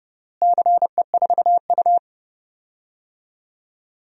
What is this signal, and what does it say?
Morse code, heard as a keyed tone, about 30 wpm, 700 Hz. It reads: CE4U
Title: DXCC-100 Call Signs - 4 Characters - Repeated 2x - 30wpm